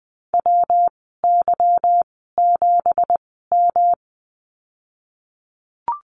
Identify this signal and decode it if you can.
Morse code, heard as a keyed tone, about 20 wpm, spaced as for 15 wpm, 700 Hz. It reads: WY7M
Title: 219 Lesson 35g - Number 0 - Call Signs 20wpm